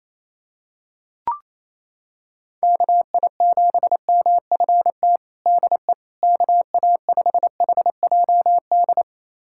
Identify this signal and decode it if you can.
Morse code, heard as a keyed tone, about 27 wpm, 700 Hz. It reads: KI7MFT DE KA5HJD
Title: USA Callsign DE Callsign - 6 Characters - 28wpm